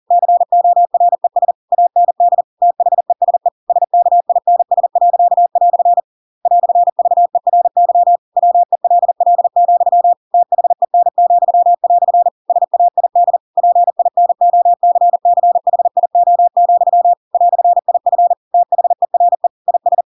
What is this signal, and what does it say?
Morse code, heard as a keyed tone, about 40 wpm, 700 Hz. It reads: CORES AND THESE SKINS." "VERY WELL, THEN," SAID PINOCCHIO, "IF THERE IS
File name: Pinocchio / Pinocchio ch7 40 40 ik1hge